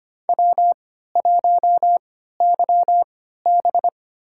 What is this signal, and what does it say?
Morse code, heard as a keyed tone, about 25 wpm, spaced as for 14 wpm, 700 Hz. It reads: W1YB
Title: USA Call Signs - 3x Character Spacing - 25wpm